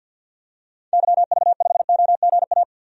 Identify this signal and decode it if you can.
Morse code, heard as a keyed tone, about 53 wpm, 700 Hz. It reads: YV5OGA
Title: DXCC-100 Call Signs - 6 Characters - 50wpm